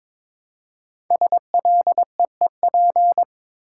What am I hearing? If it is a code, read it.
Morse code, heard as a keyed tone, about 22 wpm, 700 Hz. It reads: SLEEP
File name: 5 Letter Words - First 3 Letters Given - 22wpm